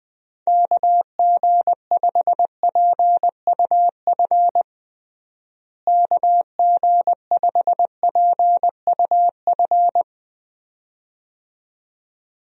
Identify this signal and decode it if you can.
Morse code, heard as a keyed tone, about 20 wpm, 700 Hz. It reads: KG5PUF KG5PUF
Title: USA Call Signs - 6 Characters - Repeated 2x - 20wpm